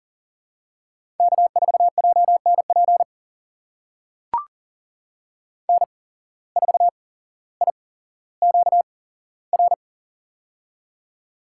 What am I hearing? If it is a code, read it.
Morse code, heard as a keyed tone, about 40 wpm, spaced as for 17 wpm, 700 Hz. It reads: K4JNP N4IQR
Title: USA Call Signs - 8x Character Spacing - 40wpm